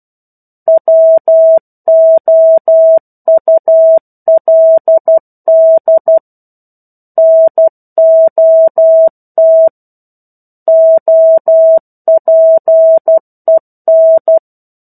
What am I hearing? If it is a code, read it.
Morse code, heard as a keyed tone, about 12 wpm, 650 Hz. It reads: WOULD NOT OPEN
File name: Chapter-01 12 WPM